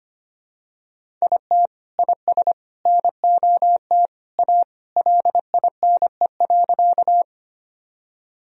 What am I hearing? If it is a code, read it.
Morse code, heard as a keyed tone, about 25 wpm, 700 Hz. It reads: IT IS NOT A LINE.
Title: Sentences from Top 1000 Words - Review Part 1 - 25wpm